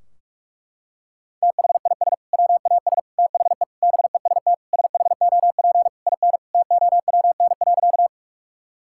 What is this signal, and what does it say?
Morse code, heard as a keyed tone, about 43 wpm, 700 Hz. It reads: THIS WAS THE BEST SHOP IN TOWN.
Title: Sentences from Top 1600 Words 45wpm